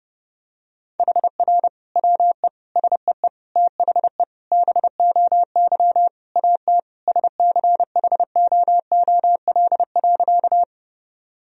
Morse code, heard as a keyed tone, about 32 wpm, 700 Hz. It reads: HR WE SEE THE BOY AT SCHOOL.
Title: Sentences from Top 300 Words and Abbreviations 30wpm